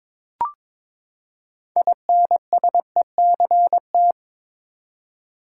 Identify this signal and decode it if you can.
Morse code, heard as a keyed tone, about 22 wpm, 700 Hz. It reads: INSECT
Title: Six Letter Words - 22wpm